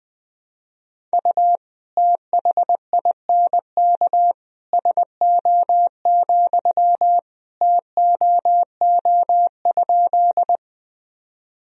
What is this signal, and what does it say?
Morse code, heard as a keyed tone, about 21 wpm, 700 Hz. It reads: U THINK SO, TOO?
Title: Sentences from Top 200 Words and Abbreviations 20wpm